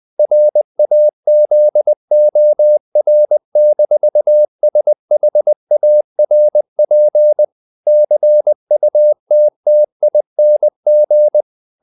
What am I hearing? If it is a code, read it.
Morse code, heard as a keyed tone, about 20 wpm, 600 Hz. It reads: RAZOR-SHARP CUTTING
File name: the old man and the sea part 2 20wpm